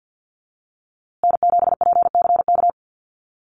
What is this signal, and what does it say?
Morse code, heard as a keyed tone, about 50 wpm, 700 Hz. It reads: N7PCK
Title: USA Call Signs Repeated 2x 50wpm